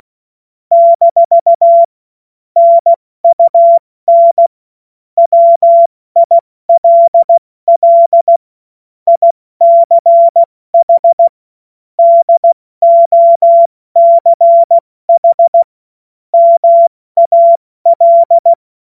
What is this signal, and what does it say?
Morse code, heard as a keyed tone, about 16 wpm, 680 Hz. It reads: - NUN WILL ICH DOCH MAL
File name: Witz-16-September-2025-70-F80